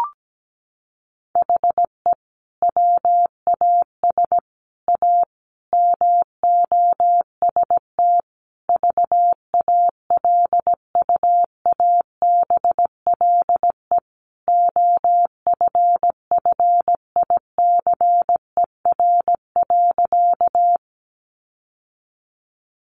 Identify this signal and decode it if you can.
Morse code, heard as a keyed tone, about 17 wpm, 700 Hz. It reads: HE WAS A MOST VALUABLE OFFICER.
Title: Sentences from Top 1200 Words 17wpm